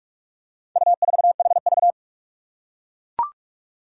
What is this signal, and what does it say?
Morse code, heard as a keyed tone, about 41 wpm, 700 Hz. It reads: U4HV
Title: Four Challenging Letter-Number Combinations 45wpm